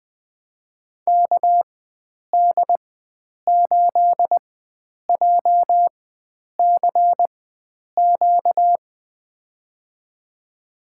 Morse code, heard as a keyed tone, about 20 wpm, spaced as for 9 wpm, 700 Hz. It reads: KD8JCQ
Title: USA Call Signs - 6 Characters - 4x Character Spacing - 20wpm